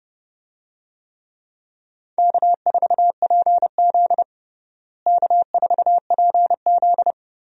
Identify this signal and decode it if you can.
Morse code, heard as a keyed tone, about 30 wpm, 700 Hz. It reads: K4PZ K4PZ
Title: USA Call Signs - 4 Characters - Repeated 3x - 30wpm